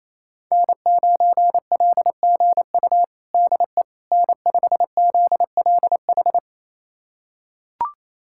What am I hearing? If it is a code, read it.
Morse code, heard as a keyed tone, about 28 wpm, 700 Hz. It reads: N9LGU DE N5ZLH
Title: USA Callsign DE Callsign - 5 Characters - 28wpm